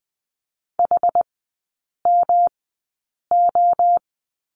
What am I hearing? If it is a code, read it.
Morse code, heard as a keyed tone, about 20 wpm, spaced as for 8 wpm, 700 Hz. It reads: HMO
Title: 107 Lesson 18e - Letter m - Three Characters 20wpm